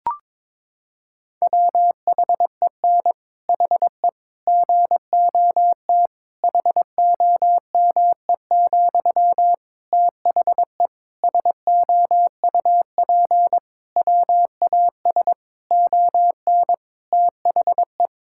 Morse code, heard as a keyed tone, about 22 wpm, 700 Hz. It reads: WHEN HE GOT HOME, THE SOUP WAS ON THE
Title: Sentences from Top 1000 Words - Review Part 6 - 22wpm